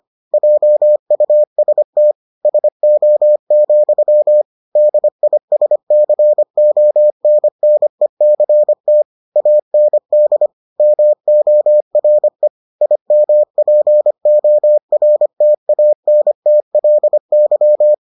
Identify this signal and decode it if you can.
Morse code, heard as a keyed tone, about 25 wpm, 600 Hz. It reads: JUST SO, DISCONNECT AND MORE IMPORTANTLY